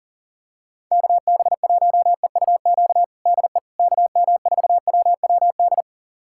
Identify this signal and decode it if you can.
Morse code, heard as a keyed tone, about 40 wpm, 700 Hz. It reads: KB1EUQ DE KM4WWD
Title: USA Callsign DE Callsign - 6 Characters - 40wpm